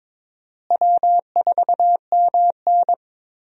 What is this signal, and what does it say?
Morse code, heard as a keyed tone, about 22 wpm, 700 Hz. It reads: W4MN
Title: USA Call Signs - 4 Characters - Repeated 3x - 22wpm